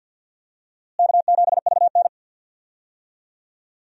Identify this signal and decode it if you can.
Morse code, heard as a keyed tone, about 51 wpm, 700 Hz. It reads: K7VN
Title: USA Call Signs - 4 Characters - Repeated 3x - 50wpm